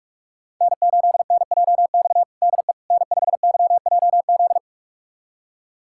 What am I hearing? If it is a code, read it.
Morse code, heard as a keyed tone, about 46 wpm, 700 Hz. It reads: N8NJX DE N5YJZ
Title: USA Callsign DE Callsign - 5 Characters - 45wpm